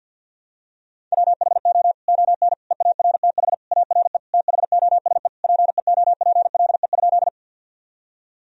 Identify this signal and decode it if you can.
Morse code, heard as a keyed tone, about 52 wpm, 700 Hz. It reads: WHO ON EARTH ARE THOSE PEOPLE?
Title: Sentences from Top 1000 Words - Review Part 1 - 50wpm